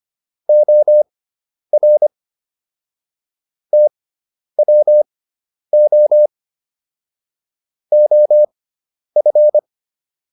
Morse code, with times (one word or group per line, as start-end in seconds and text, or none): 0.5-2.1 OR
3.7-6.3 TWO
7.9-9.6 OF